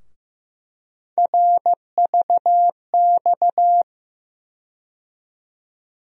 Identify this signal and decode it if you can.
Morse code, heard as a keyed tone, about 15 wpm, 700 Hz. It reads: RVX
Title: USA Call Sign Postfixes - 3 Letters 15wpm